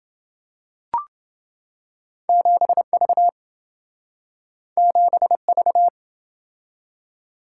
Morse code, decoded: 7V 7V

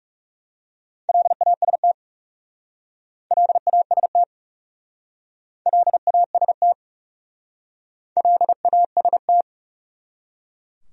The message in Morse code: LAST LAST LAST LAST

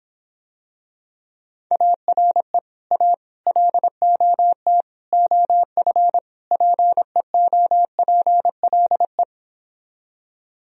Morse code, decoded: ARE A LOT OF PEOPLE

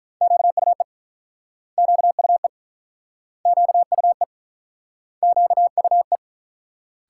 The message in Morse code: QUE QUE QUE QUE

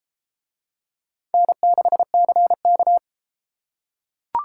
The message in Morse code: N6CK

